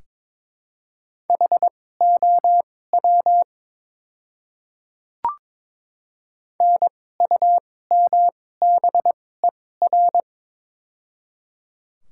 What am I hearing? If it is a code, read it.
Morse code, heard as a keyed tone, about 22 wpm, spaced as for 16 wpm, 690 Hz. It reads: HOW NUMBER
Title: 2x Character Spacing - Top 100 Words - 22wpm